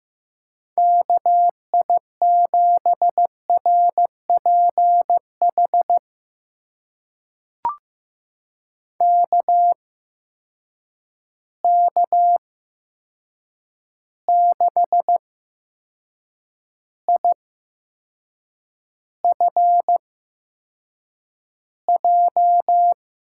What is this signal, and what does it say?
Morse code, heard as a keyed tone, about 15 wpm, spaced as for 6 wpm, 700 Hz. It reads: KI7RPH KK6IFJ
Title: USA Call Signs - 6 Characters - 8x Character Spacing - 15wpm